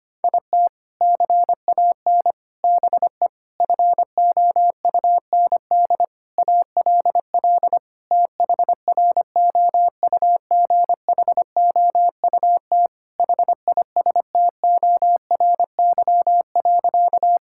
Morse code, heard as a keyed tone, about 25 wpm, 700 Hz. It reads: IT CAN BE FOUND ALL THROUGHOUT HISTORY.
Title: Sentences from Top 1000 Words - Review Part 5 - 25wpm